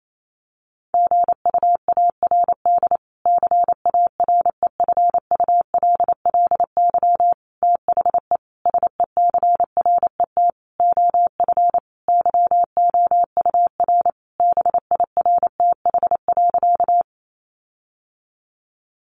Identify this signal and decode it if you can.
Morse code, heard as a keyed tone, about 28 wpm, 700 Hz. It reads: GUARD CAREFULLY THE SECRET OF YOUR BIRTH.
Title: Sentences from Top 1300 Words 28wpm